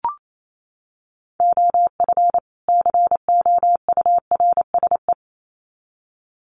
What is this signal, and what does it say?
Morse code, heard as a keyed tone, about 28 wpm, 700 Hz. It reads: OF COURSE
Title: Sets of 2 Words 28wpm